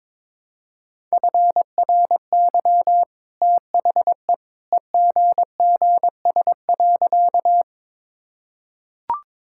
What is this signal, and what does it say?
Morse code, heard as a keyed tone, about 21 wpm, 700 Hz. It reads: FRY THE EGGS.
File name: practice / 22wpm / Easy Sentences with Hints - 22wpm